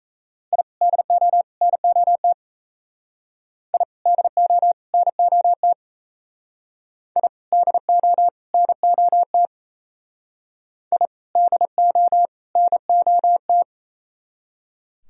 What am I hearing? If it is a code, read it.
Morse code, heard as a keyed tone, about 35 wpm, 700 Hz. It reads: I DO NOT I DO NOT I DO NOT I DO NOT
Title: Speed-Racing - Sets of 3 Words 28wpm